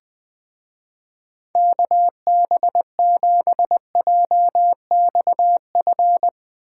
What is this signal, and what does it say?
Morse code, heard as a keyed tone, about 20 wpm, 700 Hz. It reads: KB7JXF
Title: Speed Racing - USA Call Signs 17wpm